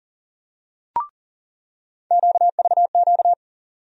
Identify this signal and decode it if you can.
Morse code, heard as a keyed tone, about 40 wpm, 700 Hz. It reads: QVQ
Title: USA Call Sign Postfixes - 3 Letters 40wpm